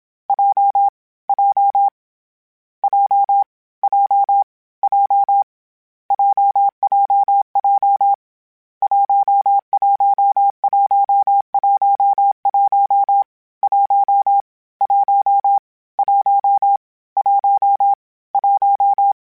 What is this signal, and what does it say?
Morse code, heard as a keyed tone, about 27 wpm, 800 Hz. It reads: J J J J J JJJ 11111 1 1 1 1 1
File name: Difficult Combos / Difficult Combos-mp3s / AWJ1 @25